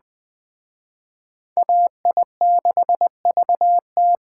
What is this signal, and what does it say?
Morse code, heard as a keyed tone, about 21 wpm, 700 Hz. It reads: AI6VT